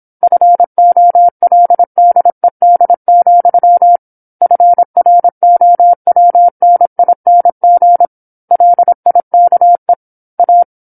Morse code, heard as a keyed tone, about 27 wpm, 700 Hz. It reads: FOLDED, FROWNING LIKE A
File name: 08-Chapter-08 26 WPM